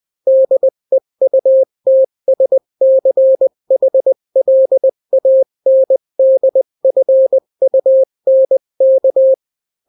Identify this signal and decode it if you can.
Morse code, heard as a keyed tone, about 20 wpm, 530 Hz. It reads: DEUTSCHLANDFUNK
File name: daily / 18wpm